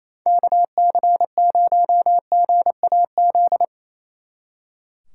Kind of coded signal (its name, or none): Morse code